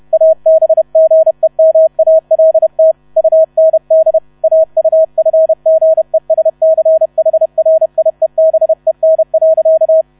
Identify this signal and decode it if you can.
Morse code, heard as a keyed tone, about 31 wpm, 640 Hz. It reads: ABGEMALT UND AUFGESCHRIEBEN.
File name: 30wpm